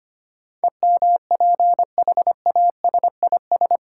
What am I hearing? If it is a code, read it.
Morse code, heard as a keyed tone, about 25 wpm, 700 Hz. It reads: EMPHASIS
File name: 8 Letter Words - First 3 Letters Given - 25wpm